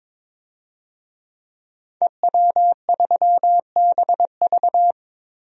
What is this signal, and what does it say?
Morse code, heard as a keyed tone, about 22 wpm, 700 Hz. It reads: EW3BV
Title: DXCC-100 Call Signs - 5 Characters - Repeated 2x - 22wpm